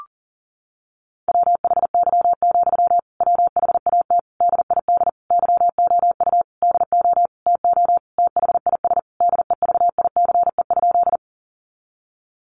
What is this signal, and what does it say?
Morse code, heard as a keyed tone, about 40 wpm, 700 Hz. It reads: WHY, WHAT DID YOU DO TO THIS DEVICE?
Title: Sentences from Top 1500 Words 40wpm